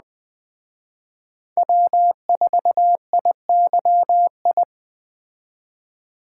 Morse code, heard as a keyed tone, about 20 wpm, 700 Hz. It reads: W4IYI
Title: Speed Racing - USA Call Signs 17wpm